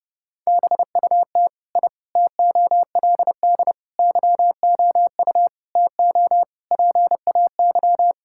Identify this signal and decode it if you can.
Morse code, heard as a keyed tone, about 32 wpm, 700 Hz. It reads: BUT I TOLD YOU TO PAY
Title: Sentences from Top 1000 Words - Review Part 6 - 30wpm